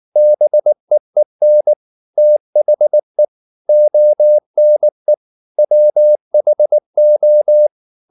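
Morse code, heard as a keyed tone, about 19 wpm, 600 Hz. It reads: BEEN THE ONE WHO
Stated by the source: Morse code at 19 wpm from CW med Gnister, reading Ken Follett's novel Never in English.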